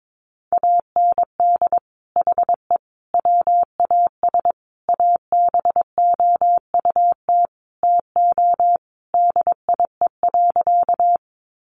Morse code, heard as a keyed tone, about 22 wpm, 700 Hz. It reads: AND HE WAS ABOUT TO DIE.
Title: Sentences from Top 1300 Words 22wpm